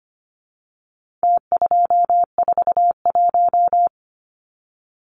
Morse code, heard as a keyed tone, about 25 wpm, 700 Hz. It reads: T241
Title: Numbers - Contest Style - 4-Digit 25wpm